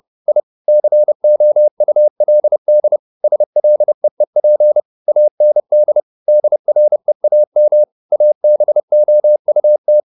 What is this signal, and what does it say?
Morse code, heard as a keyed tone, about 30 wpm, 600 Hz. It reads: I COULD SLEEP AND DREAM ABOUT
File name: the old man and the sea / the old man and the sea 1 30wpm